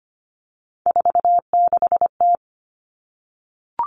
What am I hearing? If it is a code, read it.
Morse code, heard as a keyed tone, about 25 wpm, 700 Hz. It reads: 46T
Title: Three Challenging Letter-Number Combinations 25wpm